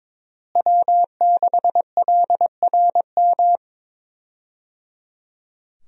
Morse code, heard as a keyed tone, about 22 wpm, 700 Hz. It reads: W6LRM
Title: USA Call Signs - 5 Characters - Repeated 3x - 22wpm